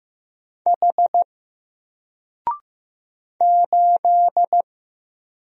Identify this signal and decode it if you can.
Morse code, heard as a keyed tone, about 15 wpm, 700 Hz. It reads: H 8
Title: Single Letter-Number Alt 15wpm